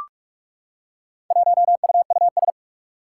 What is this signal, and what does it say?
Morse code, heard as a keyed tone, about 44 wpm, 700 Hz. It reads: 1UUS
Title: Four Challenging Letter-Number Combinations 45wpm